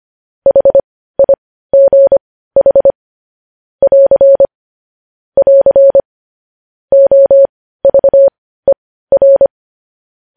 Morse code, heard as a keyed tone, about 25 wpm, spaced as for 15 wpm, 550 Hz. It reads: HIGH <AR> <AR> OVER
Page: news-headlines-in-morse-code-at-15-wpm